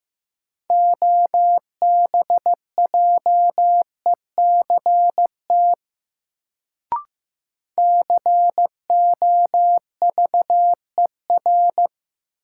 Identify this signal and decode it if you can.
Morse code, heard as a keyed tone, about 15 wpm, 700 Hz. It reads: OBJECT COVER
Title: Sentences from Top 500 Words - New Word Drill - 15wpm